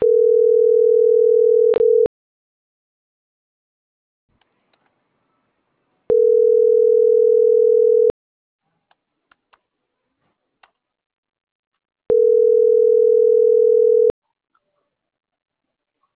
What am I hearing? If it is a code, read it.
Ringback tone (North America), 440 + 480 Hz.